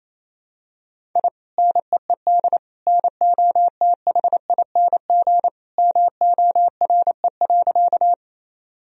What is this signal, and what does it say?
Morse code, heard as a keyed tone, about 28 wpm, 700 Hz. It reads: I NEED NOTHING MORE.
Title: Sentences from Top 1000 Words - Review Part 1 - 28wpm